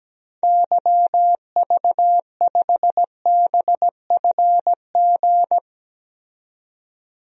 Morse code, heard as a keyed tone, about 17 wpm, 700 Hz. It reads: YV5BFG
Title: DXCC-100 Call Signs - 6 Characters - 17wpm